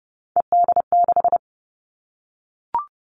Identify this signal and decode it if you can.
Morse code, heard as a keyed tone, about 30 wpm, 700 Hz. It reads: ED6
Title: Callsign Prefixes for Top 100 DXCC - 30wpm